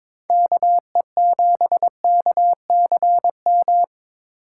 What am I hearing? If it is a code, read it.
Morse code, heard as a keyed tone, about 22 wpm, 700 Hz. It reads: KE7KCM
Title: USA Call Signs - 6 Characters - 4x Character Spacing - 22wpm